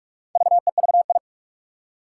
Morse code, heard as a keyed tone, about 41 wpm, 700 Hz. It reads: VEVI
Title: Four Challenging Letter-Number Combinations 45wpm